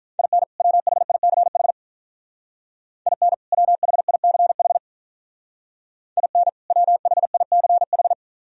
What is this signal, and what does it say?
Morse code, heard as a keyed tone, about 45 wpm, 700 Hz. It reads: IN WHICH IN WHICH IN WHICH